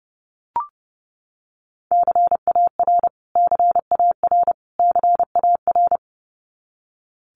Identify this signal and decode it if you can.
Morse code, heard as a keyed tone, about 30 wpm, 700 Hz. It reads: CAR CAR CAR
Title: Top 500 Words Repeated 30wpm